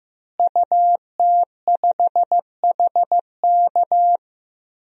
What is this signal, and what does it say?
Morse code, heard as a keyed tone, about 15 wpm, 700 Hz. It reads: UT5HK
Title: DXCC-100 Call Signs - 5 Characters - Repeated 2x - 15wpm